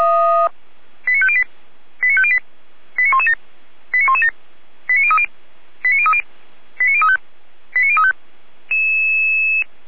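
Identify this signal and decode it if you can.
Five-tone selective call (ZVEI), heard as quick runs of five tones, tones of about 70 ms. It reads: 89508 89508 89117 89117 89030 89030 89035 89035
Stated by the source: ZVEI five-tone call 89115 received by SDR.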